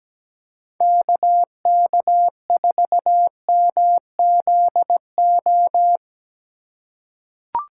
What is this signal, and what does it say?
Morse code, heard as a keyed tone, about 17 wpm, 700 Hz. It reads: KK4MZO
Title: USA Call Signs - 2x Character Spacing - 17wpm